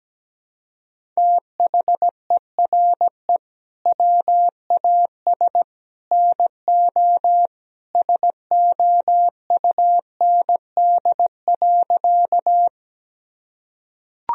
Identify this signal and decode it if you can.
Morse code, heard as a keyed tone, about 17 wpm, 700 Hz. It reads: THERE WAS NO SOUND.
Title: Sentences from Top 1000 Words - Review Part 1 - 17wpm